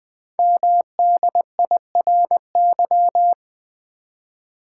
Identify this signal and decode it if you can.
Morse code, heard as a keyed tone, about 20 wpm, 700 Hz. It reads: MDIRY
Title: Code Groups - 5 Letters - 20wpm